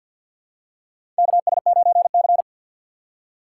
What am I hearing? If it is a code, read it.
Morse code, heard as a keyed tone, about 47 wpm, 700 Hz. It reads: KS9C
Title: USA Call Signs - 4 Characters - 3x Character Spacing - 50wpm